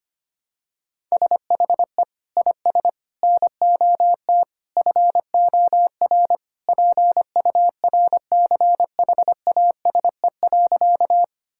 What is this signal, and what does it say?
Morse code, heard as a keyed tone, about 25 wpm, 700 Hz. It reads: SHE IS NOT FOR PURCHASE.
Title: Sentences from Top 1000 Words - Review Part 5 - 25wpm